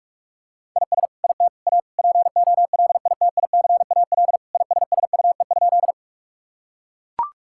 Morse code, heard as a keyed tone, about 41 wpm, 700 Hz. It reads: IS IT A POLITICAL ISSUE?
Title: Sentences from Top 1000 Words - Review Part 6 - 45wpm